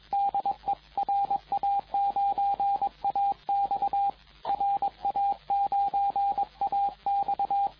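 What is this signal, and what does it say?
Morse code, heard as a keyed tone, about 22 wpm, 800 Hz, with radio noise. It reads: DE RA9AX RA9AX
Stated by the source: CW contest exchange in the 2014 Popov Cup VHF contest, RA9AX calling RA9BZ, recorded by the TR4W logger.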